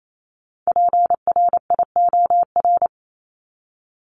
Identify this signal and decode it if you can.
Morse code, heard as a keyed tone, about 28 wpm, 700 Hz. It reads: PRIOR
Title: Top 1501-1750 Words 28wpm